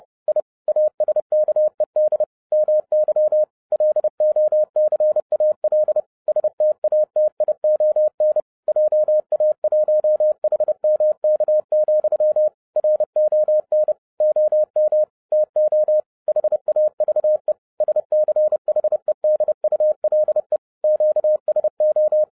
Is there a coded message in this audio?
Yes: Morse code from 0.3 to 22.2 s, heard as a keyed tone, about 30 wpm, 610 Hz. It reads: I ASKED MY LOCAL STATION JA1HMK, RON OM TO HAVE SCHEDULE QSO